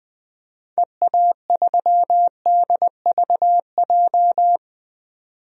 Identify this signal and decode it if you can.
Morse code, heard as a keyed tone, about 20 wpm, 700 Hz. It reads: EA3DVJ
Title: DXCC-100 Call Signs - 6 Characters - 20wpm